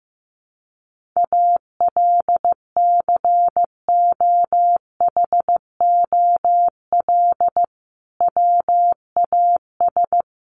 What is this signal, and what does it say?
Morse code, heard as a keyed tone, about 15 wpm, 700 Hz. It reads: ALCOHOL WAS
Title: Sentences from Top 1400 Words 15wpm